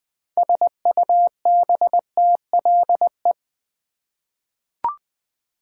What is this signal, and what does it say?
Morse code, heard as a keyed tone, about 20 wpm, 700 Hz. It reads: SUBTLE
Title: 6 Letter Words - First 3 Letters Given - 20wpm